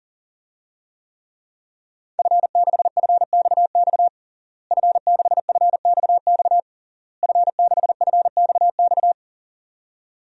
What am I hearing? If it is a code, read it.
Morse code, heard as a keyed tone, about 40 wpm, 700 Hz. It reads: F6FXX F6FXX F6FXX
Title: DXCC-100 Call Signs - 5 Characters - Repeated 3x - 40wpm